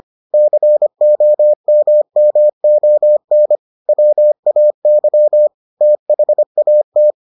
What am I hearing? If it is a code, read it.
Morse code, heard as a keyed tone, about 25 wpm, 600 Hz. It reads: COMMON WAY THAT